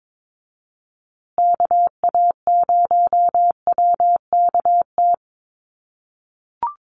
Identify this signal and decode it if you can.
Morse code, heard as a keyed tone, about 22 wpm, 700 Hz. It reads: KA0WKT
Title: USA Call Signs Repeated 2x 22wpm